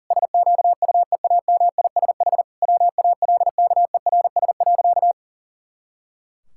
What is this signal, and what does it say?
Morse code, heard as a keyed tone, about 40 wpm, 700 Hz. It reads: SQUEAMISH WALKERS.
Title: Pangram Challenge v2 - 40wpm